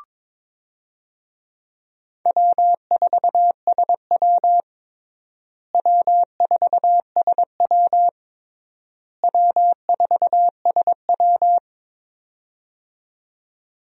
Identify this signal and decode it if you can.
Morse code, heard as a keyed tone, about 22 wpm, 700 Hz. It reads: W4SW W4SW W4SW
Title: USA Call Signs - 4 Characters - Repeated 3x - 22wpm